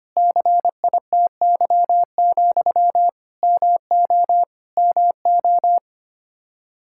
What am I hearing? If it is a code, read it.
Morse code, heard as a keyed tone, about 25 wpm, 700 Hz. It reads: CITY, MO MO
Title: Practice QSO - Exchange 1 - Beginning - 25wpm